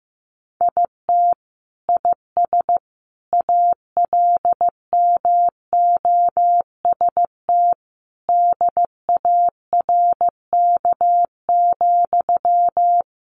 Morse code, heard as a keyed tone, about 15 wpm, 700 Hz. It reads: IT IS ALMOST DARK,
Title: Sentences from Top 1200 Words 15wpm